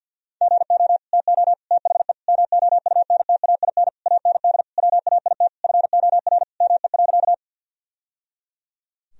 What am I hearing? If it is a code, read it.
Morse code, heard as a keyed tone, about 49 wpm, 700 Hz. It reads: GO TO THE MOUNTAIN AND WAIT FOR ME.
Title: Sentences from Top 1000 Words - Review Part 1 - 50wpm